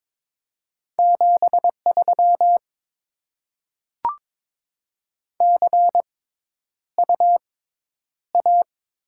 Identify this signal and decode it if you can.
Morse code, heard as a keyed tone, about 22 wpm, spaced as for 11 wpm, 700 Hz. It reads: 73 CU A
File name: Top 20 QSO Elements - 6x Character Spacing - 22wpm